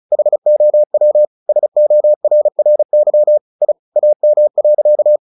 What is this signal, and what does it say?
Morse code, heard as a keyed tone, about 34 wpm, 600 Hz. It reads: HOW SORRY I AM.
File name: the old man and the sea 2 35wpm